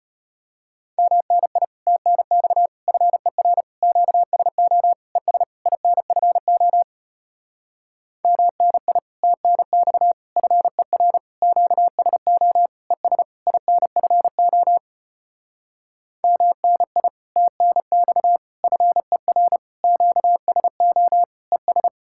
Morse code, heard as a keyed tone, about 33 wpm, 700 Hz. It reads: MNI TNX FER QSO ES INFO MNI TNX FER QSO ES INFO MNI TNX FER QSO ES